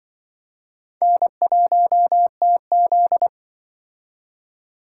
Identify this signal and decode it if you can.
Morse code, heard as a keyed tone, about 26 wpm, 700 Hz. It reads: N1TZ